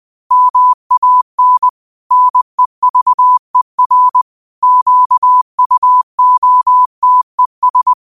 Morse code, heard as a keyed tone, about 20 wpm, 1000 Hz. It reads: MAN NEVER QUOTES